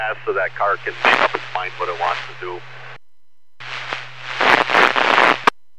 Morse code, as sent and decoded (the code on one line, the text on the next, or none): none